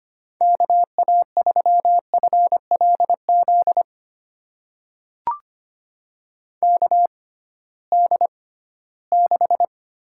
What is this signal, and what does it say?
Morse code, heard as a keyed tone, about 25 wpm, spaced as for 16 wpm, 700 Hz. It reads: KA3FLZ KD6
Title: USA Call Signs - 6 Characters - 6x Character Spacing - 25wpm